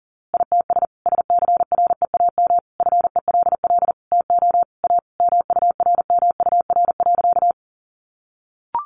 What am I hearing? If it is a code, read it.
Morse code, heard as a keyed tone, about 40 wpm, 700 Hz. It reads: ITS SCREAM FELL TO A MURMUR.
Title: A Sound Of Thunder 40wpm